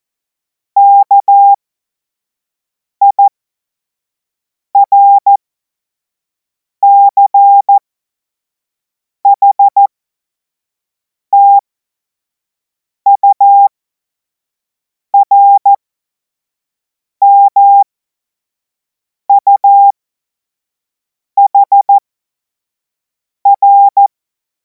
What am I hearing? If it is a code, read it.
Morse code, heard as a keyed tone, about 14 wpm, spaced as for 5 wpm, 800 Hz. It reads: KIRCHTURMUHR
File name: Morsekurs / Witze / Witz 17. Februar 2026 25 F70